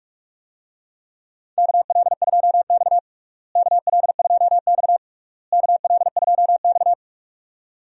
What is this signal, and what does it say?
Morse code, heard as a keyed tone, about 46 wpm, 700 Hz. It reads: KL2X KL2X KL2X
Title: USA Call Signs - 4 Characters - Repeated 3x - 45wpm